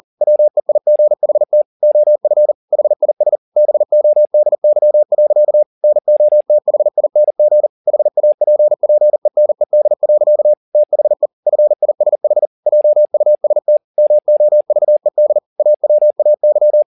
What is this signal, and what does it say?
Morse code, heard as a keyed tone, about 40 wpm, 600 Hz. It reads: WEIGHT OF HIS BODY. NOTHING HAPPENED. THE FISH JUST MOVED AWAY